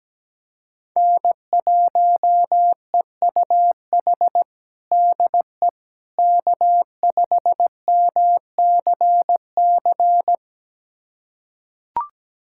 Morse code, heard as a keyed tone, about 17 wpm, 700 Hz. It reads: N1EUH DE K5MCC
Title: USA Callsign DE Callsign - 5 Characters - 17wpm